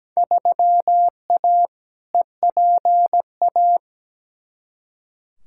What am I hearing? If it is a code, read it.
Morse code, heard as a keyed tone, about 17 wpm, 700 Hz. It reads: 3A EPA
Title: Field Day Exchanges - 17wpm